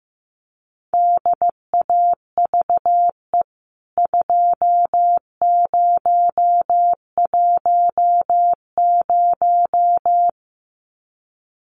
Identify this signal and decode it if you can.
Morse code, heard as a keyed tone, about 15 wpm, 700 Hz. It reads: DAVE 2010
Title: CWT Exchange 15wpm